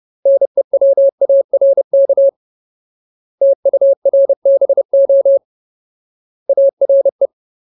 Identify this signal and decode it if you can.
Morse code, heard as a keyed tone, about 30 wpm, 550 Hz. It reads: NEWARK TURBO ARE
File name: Ultimate Word List / 30WPM Ultimate